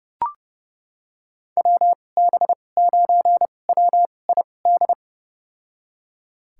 Morse code, heard as a keyed tone, about 30 wpm, spaced as for 22 wpm, 700 Hz. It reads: WB9WID
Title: USA Call Signs - 6 Characters - 2x Character Spacing - 30wpm